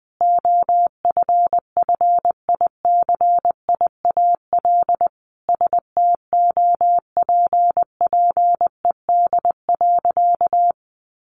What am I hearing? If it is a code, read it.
Morse code, heard as a keyed tone, about 20 wpm, 700 Hz. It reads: OFFICIAL STOPPED.
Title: A Sound Of Thunder 20wpm